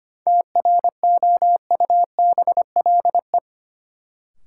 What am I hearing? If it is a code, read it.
Morse code, heard as a keyed tone, about 25 wpm, 700 Hz. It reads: TROUBLE
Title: Sentences from Top 1200 Words - New Word Drill Alt - 25wpm